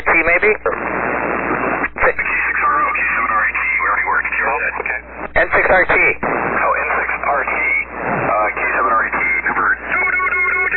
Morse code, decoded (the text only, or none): none